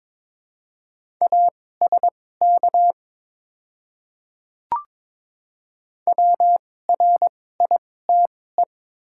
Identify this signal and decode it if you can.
Morse code, heard as a keyed tone, about 22 wpm, spaced as for 16 wpm, 700 Hz. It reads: ASK WRITE
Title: 2x Character Spacing - Top 300 Words - 22wpm